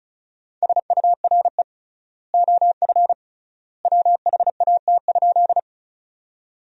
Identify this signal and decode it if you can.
Morse code, heard as a keyed tone, about 36 wpm, 700 Hz. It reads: SURE OF WHAT?
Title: 3x Word-Spacing - Sentences from Top 300 Words - 35wpm